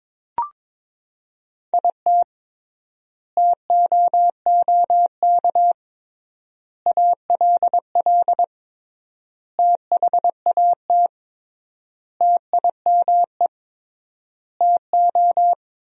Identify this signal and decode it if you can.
Morse code, heard as a keyed tone, about 21 wpm, 700 Hz. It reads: IT TOOK ALL THAT TIME TO
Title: 3x Word-Spacing - Sentences from Top 300 Words - 22wpm